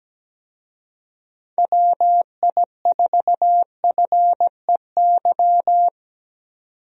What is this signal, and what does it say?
Morse code, heard as a keyed tone, about 17 wpm, 700 Hz. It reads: WI4FEY
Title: USA Call Signs - 6 Characters - 17wpm